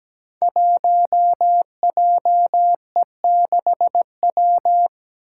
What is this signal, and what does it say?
Morse code, heard as a keyed tone, about 17 wpm, 700 Hz. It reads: 1JE6W
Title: Five Challenging Letter-Number Combinations 17wpm